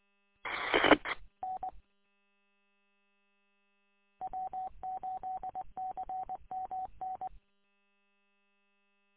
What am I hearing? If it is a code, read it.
Morse code, heard as a keyed tone, about 22 wpm, 750 Hz. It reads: A N W8CMN
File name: w8cmn- -combined